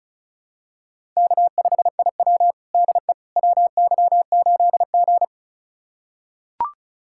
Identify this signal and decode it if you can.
Morse code, heard as a keyed tone, about 36 wpm, 700 Hz. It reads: K5IW DE WY8G
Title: USA Callsign DE Callsign - 4 Characters - 35wpm